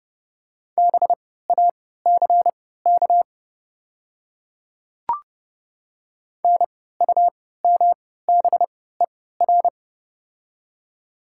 Morse code, heard as a keyed tone, about 32 wpm, spaced as for 17 wpm, 700 Hz. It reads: BACK NUMBER
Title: 3x Character Spacing - Top 200 Words - 30wpm